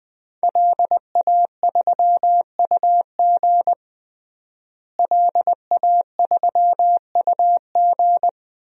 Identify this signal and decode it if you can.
Morse code, heard as a keyed tone, about 21 wpm, 700 Hz. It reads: LA3UG LA3UG
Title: DXCC-100 Call Signs - 5 Characters - Repeated 2x - 20wpm